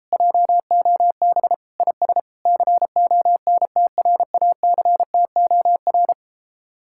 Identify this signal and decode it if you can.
Morse code, heard as a keyed tone, about 33 wpm, 700 Hz. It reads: JOB IS CONTRACTOR